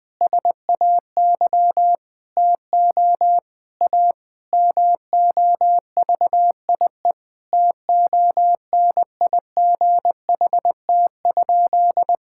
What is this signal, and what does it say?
Morse code, heard as a keyed tone, about 20 wpm, 700 Hz. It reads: SAY TO A MOVIE TONIGHT?
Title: Sentences with 5-Word Phrases - Spoken Beginning - 20wpm